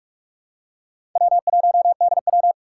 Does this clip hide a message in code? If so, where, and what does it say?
Yes: Morse code from 1.2 to 2.5 s, heard as a keyed tone, about 44 wpm, 700 Hz. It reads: W1DW